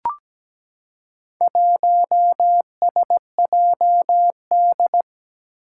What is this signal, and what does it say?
Morse code, heard as a keyed tone, about 17 wpm, 700 Hz. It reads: 1SJD
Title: Four Challenging Letter-Number Combinations 17wpm